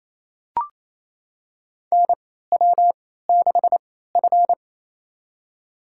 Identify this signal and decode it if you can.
Morse code, heard as a keyed tone, about 28 wpm, spaced as for 16 wpm, 700 Hz. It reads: NW6F
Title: USA Call Signs - 4 Characters - 3x Character Spacing - 28wpm